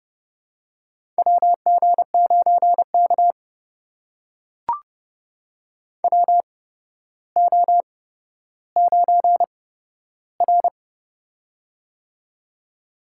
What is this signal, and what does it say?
Morse code, heard as a keyed tone, about 30 wpm, spaced as for 13 wpm, 700 Hz. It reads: WG9K WO9R
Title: USA Call Signs - 4 Characters - 8x Character Spacing - 30wpm